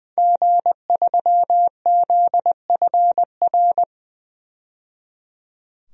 Morse code, heard as a keyed tone, about 20 wpm, 700 Hz. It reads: G3ZFR
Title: DXCC-100 Call Signs - 5 Characters - 20wpm